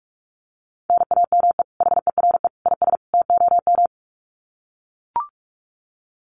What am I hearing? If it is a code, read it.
Morse code, heard as a keyed tone, about 44 wpm, 700 Hz. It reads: NAME HERE IS TOM
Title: Common QSO Phrases 45wpm